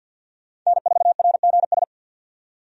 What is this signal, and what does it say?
Morse code, heard as a keyed tone, about 51 wpm, 700 Hz. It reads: N4RGS
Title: USA Call Signs - 5 Characters - 50wpm